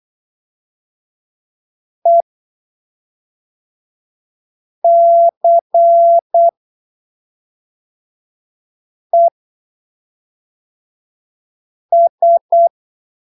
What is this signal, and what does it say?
Morse code, heard as a keyed tone, about 8 wpm, 680 Hz. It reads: E C E S
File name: arquivos / texto / texto 30 08ppm